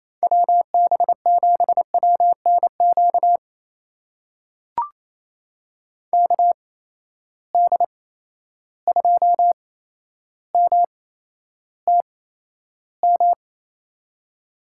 Morse code, heard as a keyed tone, about 28 wpm, spaced as for 12 wpm, 700 Hz. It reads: WB7WNQ KD2MTM
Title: USA Call Signs - 6 Characters - 8x Character Spacing - 28wpm